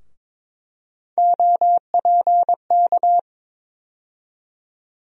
Morse code, heard as a keyed tone, about 22 wpm, 700 Hz. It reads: OPK
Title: USA Call Sign Postfixes - 3 Letters 22wpm